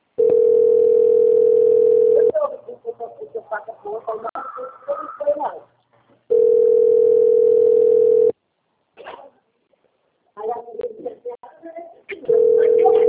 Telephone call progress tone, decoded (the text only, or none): ringback tone